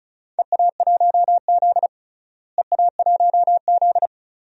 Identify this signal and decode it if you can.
Morse code, heard as a keyed tone, about 34 wpm, 700 Hz. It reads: EA1Z EA1Z
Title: DXCC-100 Call Signs - 4 Characters - Repeated 3x - 35wpm